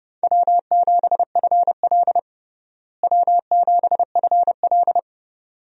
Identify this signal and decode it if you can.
Morse code, heard as a keyed tone, about 30 wpm, 700 Hz. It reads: W7FL W7FL
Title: USA Call Signs - 4 Characters - Repeated 2x - 30wpm